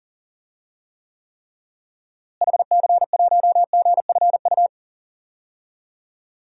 Morse code, heard as a keyed tone, about 40 wpm, 700 Hz. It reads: HC1GFU
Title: DXCC-100 Call Signs - 6 Characters - Repeated 3x - 40wpm